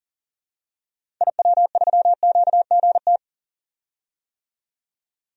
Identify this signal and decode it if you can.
Morse code, heard as a keyed tone, about 40 wpm, 700 Hz. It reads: IW3QGT
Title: DXCC-100 Call Signs - 6 Characters - 40wpm